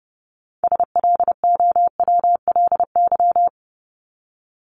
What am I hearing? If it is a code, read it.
Morse code, heard as a keyed tone, about 30 wpm, 700 Hz. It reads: SLOWLY
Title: Top 1001-1250 Words 30wpm